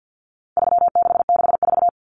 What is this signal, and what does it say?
Morse code, heard as a keyed tone, about 50 wpm, 700 Hz. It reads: 3664
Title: Common Cut Numbers - 4-Digit 50wpm